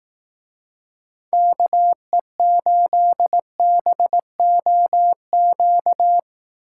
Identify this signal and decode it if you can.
Morse code, heard as a keyed tone, about 18 wpm, 700 Hz. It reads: KE8BOQ
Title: Speed Racing - USA 2x3 Call Signs 15wpm